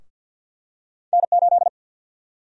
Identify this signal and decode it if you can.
Morse code, heard as a keyed tone, about 52 wpm, 700 Hz. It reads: N8
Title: USA Call Sign Prefixes - Repeated 2x - 50wpm